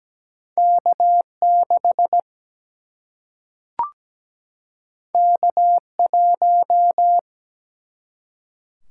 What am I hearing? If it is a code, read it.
Morse code, heard as a keyed tone, about 17 wpm, 700 Hz. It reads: K6 K1
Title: Single-Letter Single-Number 17wpm